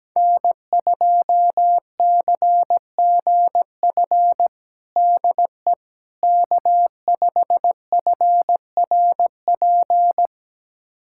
Morse code, heard as a keyed tone, about 17 wpm, 700 Hz. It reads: N2CGF DE K5FRP
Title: USA Callsign DE Callsign - 5 Characters - 17wpm